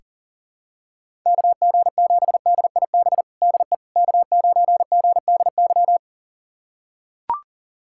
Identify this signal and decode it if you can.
Morse code, heard as a keyed tone, about 40 wpm, 700 Hz. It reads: KG7DIB DE K9GDY
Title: USA Callsign DE Callsign 40wpm